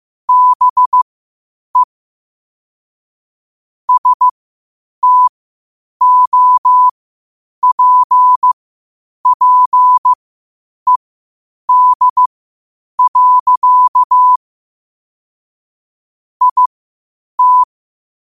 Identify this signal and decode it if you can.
Morse code, heard as a keyed tone, about 15 wpm, spaced as for 8 wpm, 1000 Hz. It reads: BE STOPPED. IT
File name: qotdfiles / 8wpm